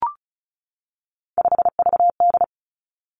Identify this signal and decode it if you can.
Morse code, heard as a keyed tone, about 36 wpm, 700 Hz. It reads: 5VD
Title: Three Challenging Letter-Number Combinations 35wpm